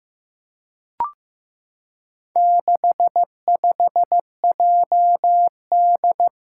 Morse code, heard as a keyed tone, about 15 wpm, 700 Hz. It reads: 65JD
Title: Four Challenging Letter-Number Combinations 15wpm